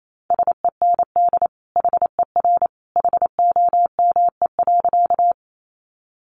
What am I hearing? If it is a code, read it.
Morse code, heard as a keyed tone, about 28 wpm, 700 Hz. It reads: SEND HER HOME.
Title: Sentences from Top 1500 Words 28wpm